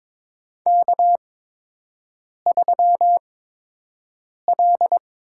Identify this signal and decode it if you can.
Morse code, heard as a keyed tone, about 22 wpm, spaced as for 6 wpm, 700 Hz. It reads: K3L
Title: USA Call Signs - 8x Character Spacing - 22wpm